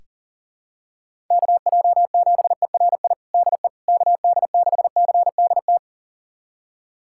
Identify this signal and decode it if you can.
Morse code, heard as a keyed tone, about 40 wpm, 700 Hz. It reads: KJ7ERI DE KD6CDT
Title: USA Callsign DE Callsign 40wpm